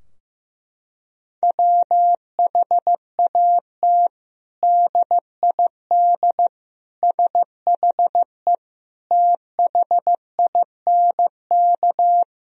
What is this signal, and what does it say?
Morse code, heard as a keyed tone, about 16 wpm, 700 Hz. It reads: WHAT DID SHE THINK
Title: Sentences from Top 1000 Words - Review Part 4 - 15wpm